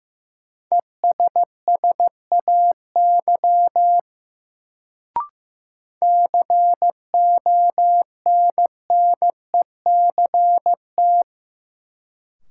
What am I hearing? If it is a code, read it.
Morse code, heard as a keyed tone, about 15 wpm, 700 Hz. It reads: ESSAY CONNECT
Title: Sentences from Top 1500 Words - New Word Drill Alt - 15wpm